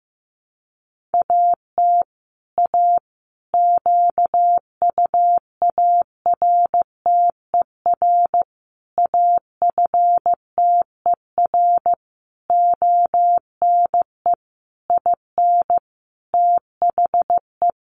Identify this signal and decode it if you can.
Morse code, heard as a keyed tone, about 15 wpm, 700 Hz. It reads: AT A QUARTER AFTER ONE IN THE
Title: Sentences from Top 1300 Words 15wpm